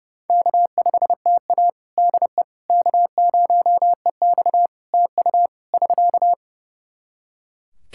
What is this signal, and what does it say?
Morse code, heard as a keyed tone, about 29 wpm, 700 Hz. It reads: K5TA DE K0EX TU <SK>
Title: Practice QSO - Exchange 4 - Ending - 30wpm